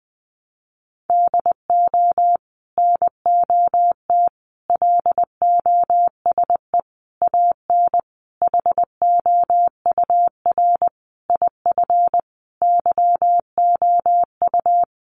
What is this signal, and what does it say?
Morse code, heard as a keyed tone, about 19 wpm, 700 Hz. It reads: DO NOT LOSE AN HOUR IF YOU
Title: Sentences from Top 1100 Words 20wpm